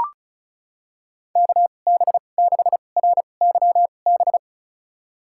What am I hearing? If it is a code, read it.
Morse code, heard as a keyed tone, about 34 wpm, spaced as for 25 wpm, 700 Hz. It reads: KB6RYB
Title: USA Call Signs - 2x Character Spacing - 35wpm